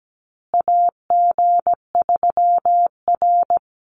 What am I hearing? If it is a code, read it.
Morse code, heard as a keyed tone, about 17 wpm, 700 Hz. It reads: AG3R
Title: Call Signs Repeated 3x - 17wpm